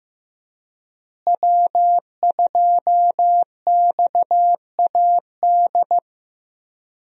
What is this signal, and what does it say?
Morse code, heard as a keyed tone, about 15 wpm, 700 Hz. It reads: W2XAD
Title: USA Call Signs - 5 Characters - Repeated 3x - 15wpm